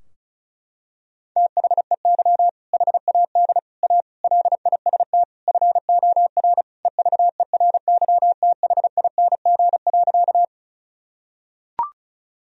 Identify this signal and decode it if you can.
Morse code, heard as a keyed tone, about 34 wpm, 700 Hz. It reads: THEY HAD A LIST FOR EVERYTHING.
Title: Sentences from Top 1000 Words - Review Part 2 - 35wpm